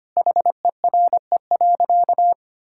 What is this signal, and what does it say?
Morse code, heard as a keyed tone, about 25 wpm, 700 Hz. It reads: HERE.
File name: practice / 25wpm / Sentences with 3-Word Phrases - 25wpm